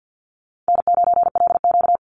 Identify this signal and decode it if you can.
Morse code, heard as a keyed tone, about 50 wpm, 700 Hz. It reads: N9LQ
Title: Call Signs Repeated 2x - 50wpm